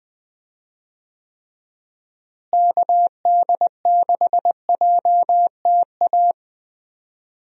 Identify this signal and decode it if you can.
Morse code, heard as a keyed tone, about 21 wpm, 700 Hz. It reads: KD6JTA
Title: USA Call Signs - 6 Characters - Repeated 3x - 20wpm